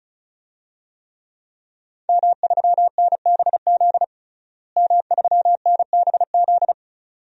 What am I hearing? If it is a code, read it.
Morse code, heard as a keyed tone, about 36 wpm, 700 Hz. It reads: M3NBZ M3NBZ
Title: DXCC-100 Call Signs - 5 Characters - Repeated 2x - 35wpm